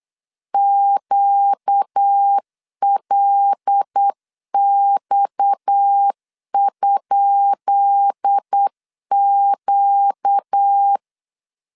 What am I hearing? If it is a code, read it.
Morse code, heard as a keyed tone, about 8 wpm, 800 Hz. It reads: QLX?Q
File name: LEZIONI / 10 SIGNS / 10B INT 8wpm 3534